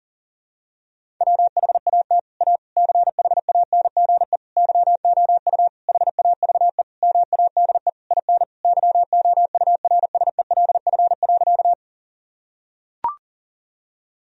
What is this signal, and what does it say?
Morse code, heard as a keyed tone, about 37 wpm, 700 Hz. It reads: WHAT A CHANGE YOU HAVE MADE IN YOURSELF.
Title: Sentences from Top 1000 Words - Review Part 1 - 40wpm